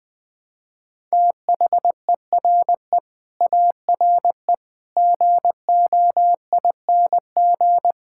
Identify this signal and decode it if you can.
Morse code, heard as a keyed tone, about 20 wpm, 700 Hz. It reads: THERE ARE GOING
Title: Sentences with 8-Word Phrases - Spoken Beginning - 20wpm